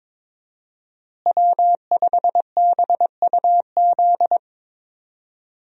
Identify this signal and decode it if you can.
Morse code, heard as a keyed tone, about 22 wpm, 700 Hz. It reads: W5BUZ
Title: USA Call Signs - 5 Characters - Repeated 2x - 22wpm